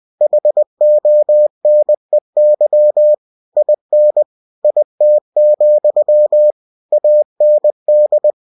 Morse code, heard as a keyed tone, about 20 wpm, 600 Hz. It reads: HONEY IN IT, AND
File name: cw-books / winnie-the-pooh / winnie-20wpm-page-0050